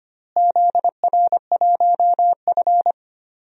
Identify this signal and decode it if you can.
Morse code, heard as a keyed tone, about 25 wpm, 700 Hz. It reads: ZR1F
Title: DXCC-100 Call Signs - 4 Characters - Repeated 2x - 25wpm